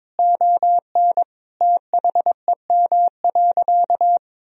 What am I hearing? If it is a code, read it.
Morse code, heard as a keyed tone, about 22 wpm, 700 Hz. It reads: ON THEM.
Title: Sentences from Top 1000 Words - Review Part 3 - 22wpm